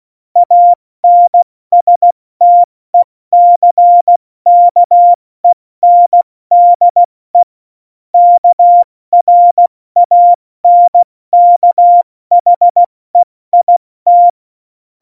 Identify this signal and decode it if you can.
Morse code, heard as a keyed tone, about 16 wpm, 700 Hz. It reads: ANSTECKENDE KRANKHEIT
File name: Morsekurs / Witze / Witz 02. Januar 2018, 70 F80